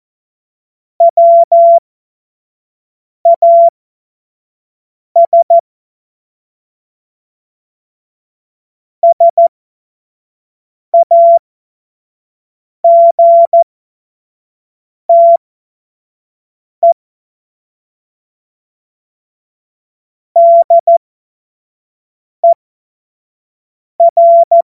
Morse code, heard as a keyed tone, about 14 wpm, spaced as for 5 wpm, 680 Hz. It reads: WAS SAGTE DER
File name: Morsekurs / Witze / Witz 07.November 2023 25 F70